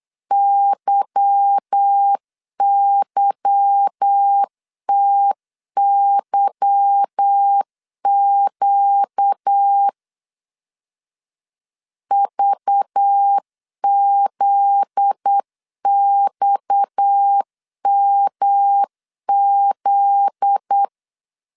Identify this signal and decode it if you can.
Morse code, heard as a keyed tone, about 8 wpm, 800 Hz. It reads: YYTYQ VZXMZ
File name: LEZIONI / 8 YJ45 / 8 Y 8wpm 3245